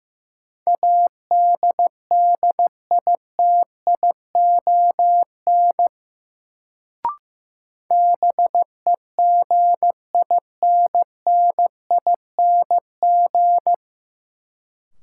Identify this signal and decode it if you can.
Morse code, heard as a keyed tone, about 15 wpm, 700 Hz. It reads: ADDITION BEGINNING
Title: Sets of 3 Words - New Word Drill - 15wpm